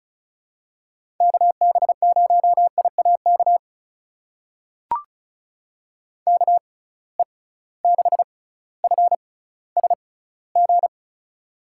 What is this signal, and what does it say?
Morse code, heard as a keyed tone, about 36 wpm, spaced as for 18 wpm, 700 Hz. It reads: KB0IAK KE6FSG
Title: USA Call Signs - 6 Characters - 6x Character Spacing - 35wpm